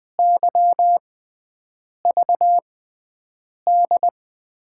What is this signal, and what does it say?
Morse code, heard as a keyed tone, about 21 wpm, spaced as for 7 wpm, 700 Hz. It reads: YVD
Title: USA Call Signs - 6 Characters - 6x Character Spacing - 20wpm